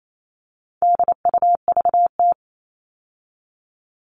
Morse code, heard as a keyed tone, about 28 wpm, 700 Hz. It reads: DUVT
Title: All Cut Numbers - 4-Digit 28wpm